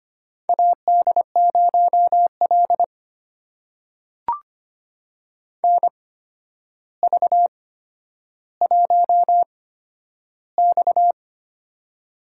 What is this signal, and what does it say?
Morse code, heard as a keyed tone, about 25 wpm, spaced as for 11 wpm, 700 Hz. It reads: AD0L NV1X
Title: USA Call Signs - 4 Characters - 8x Character Spacing - 25wpm